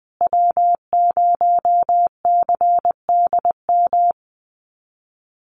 Morse code, heard as a keyed tone, about 20 wpm, 700 Hz. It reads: W0CDM
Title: USA Call Signs Repeated 3x 20wpm